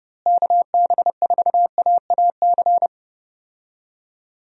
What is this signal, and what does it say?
Morse code, heard as a keyed tone, about 30 wpm, 700 Hz. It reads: KB4AAC
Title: USA Call Signs - 6 Characters - Repeated 2x - 30wpm